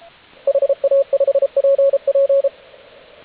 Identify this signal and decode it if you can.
Morse code, heard as a keyed tone, about 33 wpm, 550 Hz. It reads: HA5PP
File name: AUDIO2019 / HG8DX CW